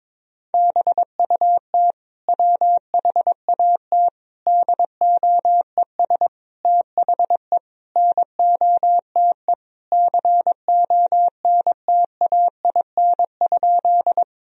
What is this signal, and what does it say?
Morse code, heard as a keyed tone, about 22 wpm, 700 Hz. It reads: BUT WHAT DOES THE NOTE CONTAIN?
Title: Sentences from Top 1000 Words - Review Part 2 - 22wpm